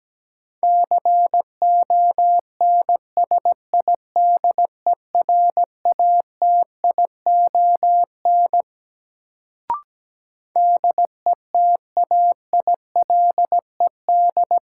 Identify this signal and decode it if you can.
Morse code, heard as a keyed tone, about 17 wpm, 700 Hz. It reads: CONSIDERATION DETAILED
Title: Sentences from Top 1500 Words - New Word Drill - 17wpm